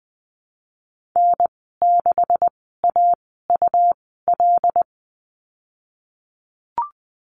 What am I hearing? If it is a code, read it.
Morse code, heard as a keyed tone, about 20 wpm, spaced as for 15 wpm, 700 Hz. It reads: N6AUL
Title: 173 Lesson 28g - Number 9 - Call Signs 20wpm